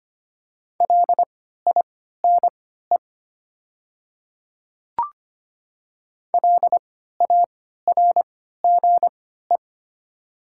Morse code, heard as a keyed tone, about 25 wpm, spaced as for 14 wpm, 700 Hz. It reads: LINE LARGE
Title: 3x Character Spacing - Top 300 Words - 25wpm